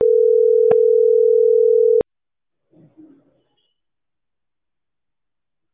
Ringback tone (North America), 440 + 480 Hz.